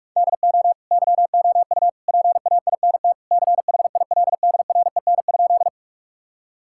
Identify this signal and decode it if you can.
Morse code, heard as a keyed tone, about 42 wpm, 700 Hz. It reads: DO YOU PAINT CHILDREN?
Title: Sentences from Top 1000 Words - Review Part 6 - 45wpm